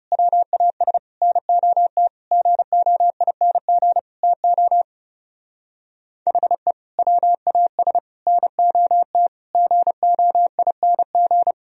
Morse code, heard as a keyed tone, about 37 wpm, 700 Hz. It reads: WAS NOT GOING TO HE WAS NOT GOING
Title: Speed-Racing - Sets of 5 Words 30wpm